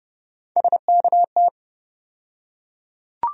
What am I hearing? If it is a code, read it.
Morse code, heard as a keyed tone, about 30 wpm, 700 Hz. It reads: SKT
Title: USA Call Sign Postfixes - 3 Letters 30wpm